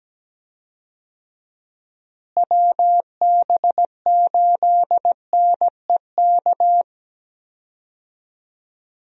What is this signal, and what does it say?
Morse code, heard as a keyed tone, about 17 wpm, 700 Hz. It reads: WB8NEK
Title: USA Call Signs - 6 Characters - Repeated 2x - 17wpm